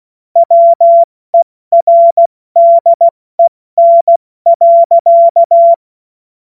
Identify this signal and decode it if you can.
Morse code, heard as a keyed tone, about 16 wpm, spaced as for 14 wpm, 680 Hz. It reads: WERDEN.
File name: Morsekurs / Witze / Witz 21. Juni 2022 70 F80